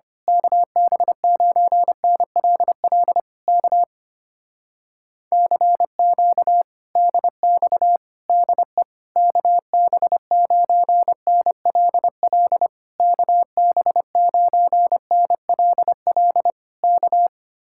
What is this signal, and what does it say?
Morse code, heard as a keyed tone, about 24 wpm, 700 Hz. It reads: KB9NLL K CQ DX DE KB9NLL KB9NLL K
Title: Speed-Racing Common QSO Phrases - 25wpm